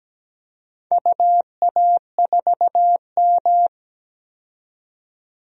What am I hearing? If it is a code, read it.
Morse code, heard as a keyed tone, about 17 wpm, 700 Hz. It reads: UA4M
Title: DXCC-100 Call Signs - 4 Characters - 17wpm